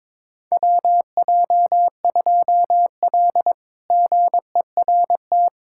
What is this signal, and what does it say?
Morse code, heard as a keyed tone, about 22 wpm, 700 Hz. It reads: WJ2L GERT
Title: K1USN SST QSO Practice - 22wpm